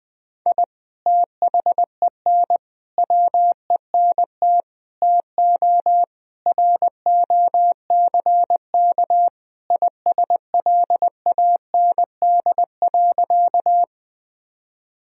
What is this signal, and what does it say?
Morse code, heard as a keyed tone, about 21 wpm, 700 Hz. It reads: I THEN WENT TO ROCK ISLAND.
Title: Sentences from Top 1000 Words - Review Part 3 - 20wpm